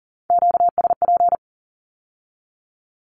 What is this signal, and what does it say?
Morse code, heard as a keyed tone, about 42 wpm, 700 Hz. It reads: QSP
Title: Q-Codes with Spoken Meaning - 40wpm